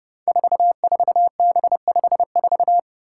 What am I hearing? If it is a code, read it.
Morse code, heard as a keyed tone, about 30 wpm, 700 Hz. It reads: 44B54
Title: Five Challenging Letter-Number Combinations 30wpm